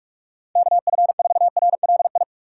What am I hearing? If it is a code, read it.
Morse code, heard as a keyed tone, about 41 wpm, 700 Hz. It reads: KF4RLI